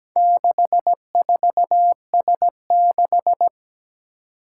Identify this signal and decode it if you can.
Morse code, heard as a keyed tone, about 17 wpm, 700 Hz. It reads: 64S6
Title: Four Challenging Letter-Number Combinations 17wpm